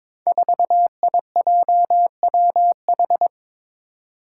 Morse code, heard as a keyed tone, about 22 wpm, 700 Hz. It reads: 4IJWH
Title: Five Challenging Letter-Number Combinations 22wpm